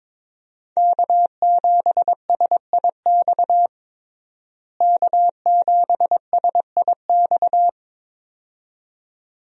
Morse code, heard as a keyed tone, about 22 wpm, 700 Hz. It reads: K7SIX K7SIX
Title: USA Call Signs - 5 Characters - Repeated 2x - 22wpm